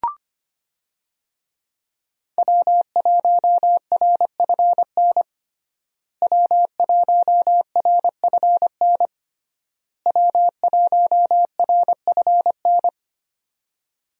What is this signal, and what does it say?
Morse code, heard as a keyed tone, about 25 wpm, 700 Hz. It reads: W1RFN W1RFN W1RFN
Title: USA Call Signs - 5 Characters - Repeated 3x - 25wpm